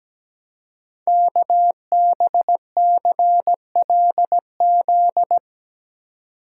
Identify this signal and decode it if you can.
Morse code, heard as a keyed tone, about 17 wpm, 700 Hz. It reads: KBCLZ
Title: Code Groups - 5 Letters - 17wpm